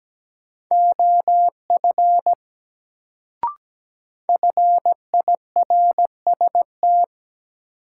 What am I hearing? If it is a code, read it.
Morse code, heard as a keyed tone, about 17 wpm, 700 Hz. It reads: OF FIRST
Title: Sentences from Top 100 Words - New Word Drill - 17wpm